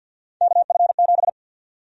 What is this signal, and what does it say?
Morse code, heard as a keyed tone, about 52 wpm, 700 Hz. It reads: KF7
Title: USA Call Sign Prefixes - 50wpm